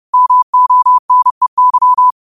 Morse code, heard as a keyed tone, about 30 wpm, 1000 Hz. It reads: MONEY
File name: qotdfiles / 30wpm